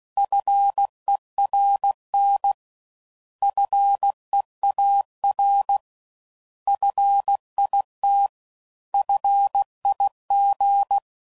Morse code, heard as a keyed tone, about 16 wpm, 800 Hz. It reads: FERN FEAR FIT FIG